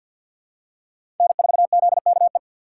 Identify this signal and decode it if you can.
Morse code, heard as a keyed tone, about 53 wpm, 700 Hz. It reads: N4ZKE
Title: USA Call Signs - 5 Characters - 3x Character Spacing - 50wpm